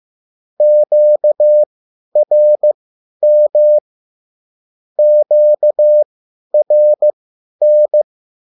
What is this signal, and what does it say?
Morse code, heard as a keyed tone, about 15 wpm, spaced as for 10 wpm, 600 Hz. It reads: QRM QRN
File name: QSL / CW lesson